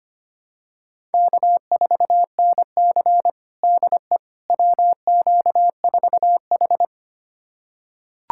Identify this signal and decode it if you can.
Morse code, heard as a keyed tone, about 25 wpm, 700 Hz. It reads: K4NC DE WQ4H
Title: USA Callsign DE Callsign - 4 Characters - 25wpm